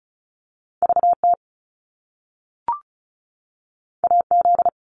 Morse code, heard as a keyed tone, about 36 wpm, 700 Hz. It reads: VT AZ
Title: US States 35wpm